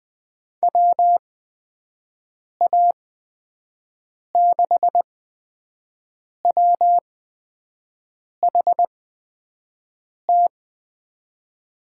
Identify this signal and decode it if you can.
Morse code, heard as a keyed tone, about 20 wpm, spaced as for 5 wpm, 700 Hz. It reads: WA6WHT
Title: USA Call Signs - 6 Characters - 8x Character Spacing - 20wpm